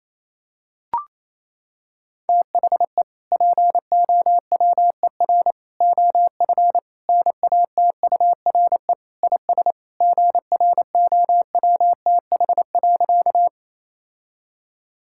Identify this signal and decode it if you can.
Morse code, heard as a keyed tone, about 28 wpm, 700 Hz. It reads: THE POWER OF NATURE IS GROWTH.
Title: Sentences from Top 1000 Words - Review Part 5 - 28wpm